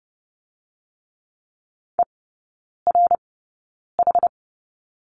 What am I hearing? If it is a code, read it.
Morse code, heard as a keyed tone, about 30 wpm, spaced as for 9 wpm, 700 Hz. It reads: ERH
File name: Lesson 16e - Letter c - Three Characters 30wpm